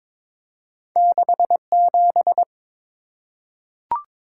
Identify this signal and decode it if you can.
Morse code, heard as a keyed tone, about 22 wpm, 690 Hz. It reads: 67
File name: practice / 22wpm / Two Letter-Number Combinations - 22wpm